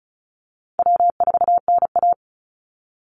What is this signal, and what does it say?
Morse code, heard as a keyed tone, about 34 wpm, 700 Hz. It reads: W4NA
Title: Easy Call Signs Repeated 3x - 35wpm